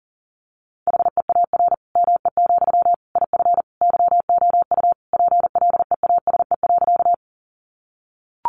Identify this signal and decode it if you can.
Morse code, heard as a keyed tone, about 40 wpm, 700 Hz. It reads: HEAR ME, IF YOU PLEASE.
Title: Sentences from Top 1500 Words 40wpm